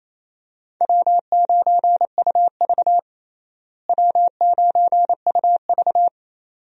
Morse code, heard as a keyed tone, about 27 wpm, 700 Hz. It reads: W9UV W9UV
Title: USA Call Signs - 4 Characters - Repeated 2x - 28wpm